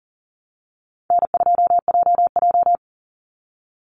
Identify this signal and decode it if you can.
Morse code, heard as a keyed tone, about 40 wpm, 700 Hz. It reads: N2JJ
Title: Easy Call Signs Repeated 3x - 40wpm